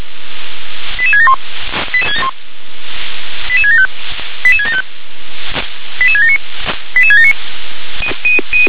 Five-tone selective call (ZVEI), heard as quick runs of five tones, tones of about 80 ms. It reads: 88671 88675 88675 88679 88679
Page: SDR/ZVEI